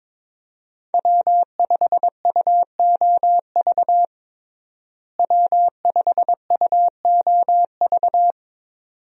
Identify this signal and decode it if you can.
Morse code, heard as a keyed tone, about 22 wpm, 700 Hz. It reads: W5UOV W5UOV
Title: USA Call Signs - 5 Characters - Repeated 2x - 22wpm